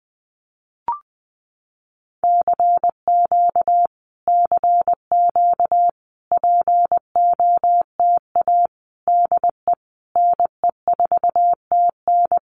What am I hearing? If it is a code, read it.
Morse code, heard as a keyed tone, about 20 wpm, 700 Hz. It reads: CQ CQ POTA DE NE4TN
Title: Common QSO Phrases Encore 20wpm